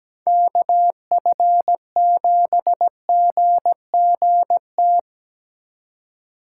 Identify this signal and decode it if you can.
Morse code, heard as a keyed tone, about 19 wpm, 700 Hz. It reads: KF7GGT
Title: Speed Racing - USA 2x3 Call Signs 17wpm